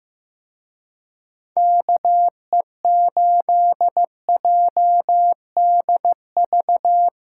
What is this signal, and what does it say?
Morse code, heard as a keyed tone, about 16 wpm, 700 Hz. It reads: KE8JDV
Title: Speed Racing - USA 2x3 Call Signs 15wpm